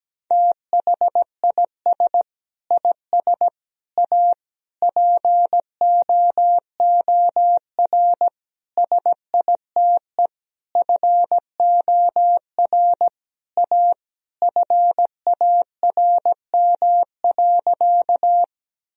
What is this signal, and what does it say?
Morse code, heard as a keyed tone, about 17 wpm, 700 Hz. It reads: THIS IS A POOR SITE FOR A FARM.
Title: Sentences from Top 1000 Words - Review Part 4 - 17wpm